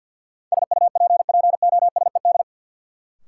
Morse code, heard as a keyed tone, about 47 wpm, 700 Hz. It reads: SUPPOSED
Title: Sets of 3 Words - New Word Drill - 50wpm